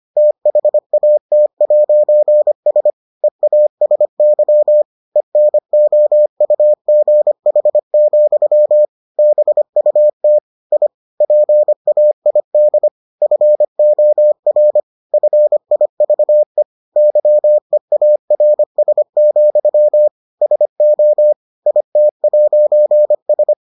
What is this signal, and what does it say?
Morse code, heard as a keyed tone, about 25 wpm, 600 Hz. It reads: THAT'S EASY ENOUGH, BUT I PAID FOR FIVE YEARS, SO IT'S